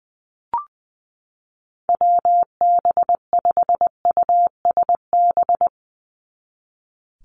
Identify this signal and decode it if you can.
Morse code, heard as a keyed tone, about 20 wpm, 700 Hz. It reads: WB5USB
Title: NA SOTA Chaser Callsigns 20wpm